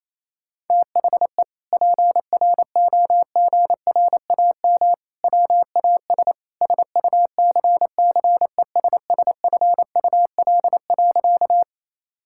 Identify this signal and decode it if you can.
Morse code, heard as a keyed tone, about 28 wpm, 700 Hz. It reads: THE PROGRAM WAS SUCCESSFUL.
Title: Sentences from Top 1000 Words - Review Part 5 - 28wpm